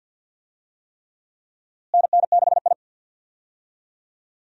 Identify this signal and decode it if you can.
Morse code, heard as a keyed tone, about 47 wpm, 700 Hz. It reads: NN6I
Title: USA Call Signs - 4 Characters - Repeated 3x - 50wpm